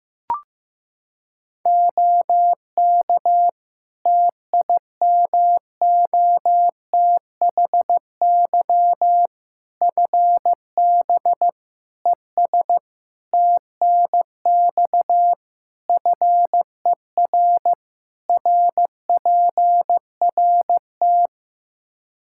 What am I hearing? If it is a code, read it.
Morse code, heard as a keyed tone, about 15 wpm, 700 Hz. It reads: OK TIMOTHY FB ES TNX FER RPRT
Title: Practice QSO - Exchange 2 - Info - 15wpm